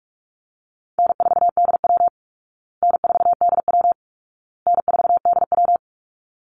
Morse code, heard as a keyed tone, about 44 wpm, 700 Hz. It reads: N4DW N4DW N4DW
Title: Call Signs Repeated 2x - 45wpm